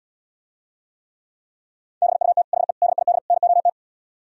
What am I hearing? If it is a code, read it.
Morse code, heard as a keyed tone, about 77 wpm, 700 Hz. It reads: BUT HE DID NOT